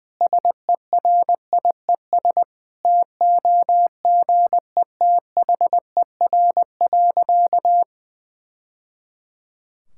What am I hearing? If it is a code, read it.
Morse code, heard as a keyed tone, about 20 wpm, 700 Hz. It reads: SERIES TOGETHER.
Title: Sentences from Top 1000 Words - Review Part 5 - 20wpm